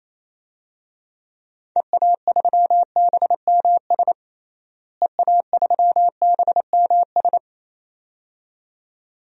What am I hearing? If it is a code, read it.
Morse code, heard as a keyed tone, about 28 wpm, 700 Hz. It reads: EA3BMS EA3BMS
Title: DXCC-100 Call Signs - 6 Characters - Repeated 2x - 28wpm